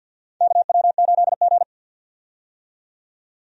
Morse code, heard as a keyed tone, about 52 wpm, 700 Hz. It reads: KW8G
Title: USA Call Signs - 4 Characters - 50wpm